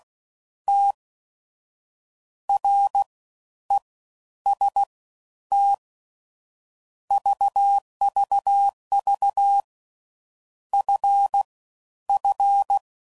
Morse code, heard as a keyed tone, about 16 wpm, 800 Hz. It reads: T R E S T VVV F F